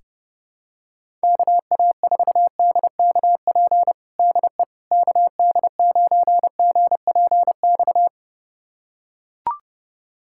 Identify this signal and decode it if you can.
Morse code, heard as a keyed tone, about 32 wpm, 700 Hz. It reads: KA4DKP DE KD9GPX
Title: USA Callsign DE Callsign 30wpm